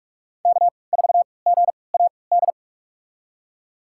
Morse code, heard as a keyed tone, about 46 wpm, spaced as for 26 wpm, 700 Hz. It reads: K4GAD
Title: USA Call Signs - 5 Characters - 3x Character Spacing - 45wpm